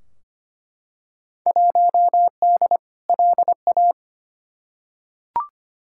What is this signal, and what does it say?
Morse code, heard as a keyed tone, about 25 wpm, 700 Hz. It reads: 1D LA
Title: Field Day Exchanges - 25wpm